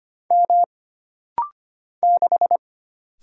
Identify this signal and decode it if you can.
Morse code, heard as a keyed tone, about 25 wpm, 700 Hz. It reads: M 6
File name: Single Character (Letter:Number:Punctuation:Prosign) - Rapid-Fire - 25wpm